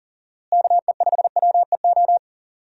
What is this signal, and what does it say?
Morse code, heard as a keyed tone, about 40 wpm, 700 Hz. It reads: KE5WEO
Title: USA Call Signs - 6 Characters - Repeated 2x - 40wpm